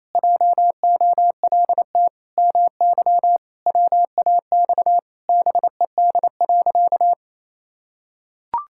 Morse code, heard as a keyed tone, about 28 wpm, 700 Hz. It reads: JOLT MY WAX BED.
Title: Pangram Challenge v2 - 28wpm